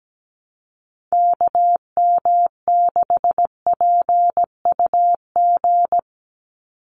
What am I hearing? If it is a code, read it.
Morse code, heard as a keyed tone, about 17 wpm, 700 Hz. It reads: KM6PUG
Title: USA Call Signs 17wpm